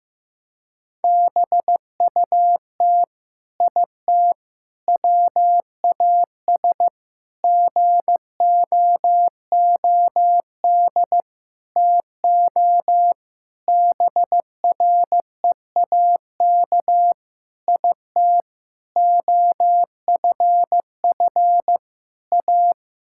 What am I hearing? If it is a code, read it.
Morse code, heard as a keyed tone, about 15 wpm, 700 Hz. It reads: BUT IT WAS GOOD TO BREAK IT OFF A
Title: Sentences from Top 1000 Words - Review Part 6 - 15wpm